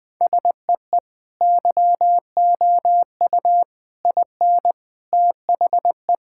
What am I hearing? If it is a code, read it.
Morse code, heard as a keyed tone, about 20 wpm, 700 Hz. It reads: SEE YOU IN THE
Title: Sentences with 3-Word Phrases - 20wpm